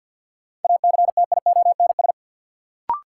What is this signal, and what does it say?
Morse code, heard as a keyed tone, about 53 wpm, 700 Hz. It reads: ACTIONS
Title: Sentences from Top 1000 Words - Review Part 6 - New Word Drill - 50wpm